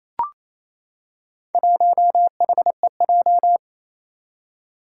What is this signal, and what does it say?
Morse code, heard as a keyed tone, about 28 wpm, 700 Hz. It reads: 1HEJ
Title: Four Challenging Letter-Number Combinations 28wpm